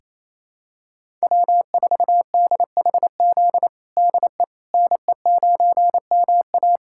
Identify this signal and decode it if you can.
Morse code, heard as a keyed tone, about 27 wpm, 700 Hz. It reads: W4DHZ DE NE9MA
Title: USA Callsign DE Callsign - 5 Characters - 28wpm